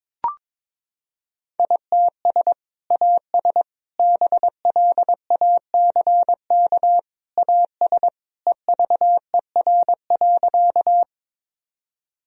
Morse code, heard as a keyed tone, about 22 wpm, 700 Hz. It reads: ITS AS BLACK AS EVER.
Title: Sentences from Top 1000 Words - Review Part 2 - 22wpm